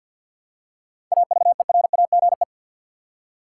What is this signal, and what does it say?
Morse code, heard as a keyed tone, about 52 wpm, 700 Hz. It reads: AVERAGE
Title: Seven Letter Words - 50wpm